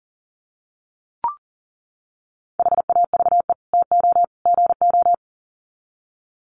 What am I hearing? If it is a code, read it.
Morse code, heard as a keyed tone, about 37 wpm, 700 Hz. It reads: HAVE TO GO
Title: Sets of 3 Words 40wpm